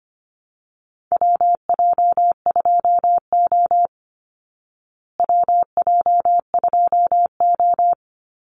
Morse code, heard as a keyed tone, about 25 wpm, 700 Hz. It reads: WJ2O WJ2O
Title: Call Signs Repeated 3x - 25wpm